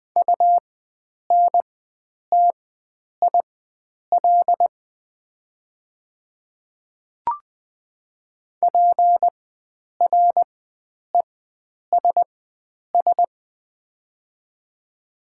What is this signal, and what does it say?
Morse code, heard as a keyed tone, about 20 wpm, spaced as for 9 wpm, 700 Hz. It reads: UNTIL PRESS
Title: 4x Character Spacing - Top 300 Words - 20wpm